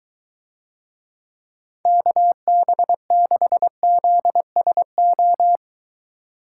Morse code, heard as a keyed tone, about 23 wpm, 700 Hz. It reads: KB6ZSO